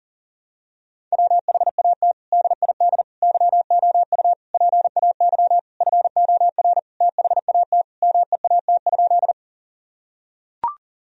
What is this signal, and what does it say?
Morse code, heard as a keyed tone, about 40 wpm, 700 Hz. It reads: WHAT DID YOU PAY FOR THAT MEAT?
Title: Sentences from Top 1000 Words - Review Part 3 - 40wpm